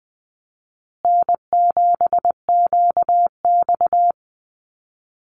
Morse code, heard as a keyed tone, about 20 wpm, 700 Hz. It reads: N7QX
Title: USA Call Signs Repeated 3x 20wpm